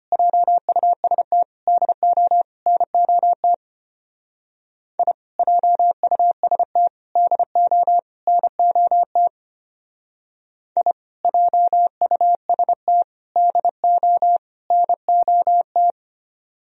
Morse code, heard as a keyed tone, about 31 wpm, 700 Hz. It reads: JUST DO NOT I JUST DO NOT I JUST DO NOT
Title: Speed-Racing - Sets of 4 Words 25wpm